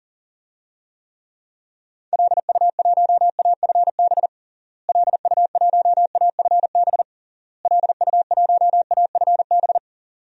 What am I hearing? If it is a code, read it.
Morse code, heard as a keyed tone, about 40 wpm, 700 Hz. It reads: LU1AFB LU1AFB LU1AFB
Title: DXCC-100 Call Signs - 6 Characters - Repeated 3x - 40wpm